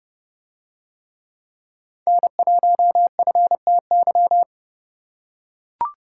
Morse code, heard as a keyed tone, about 29 wpm, 700 Hz. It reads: N1FTY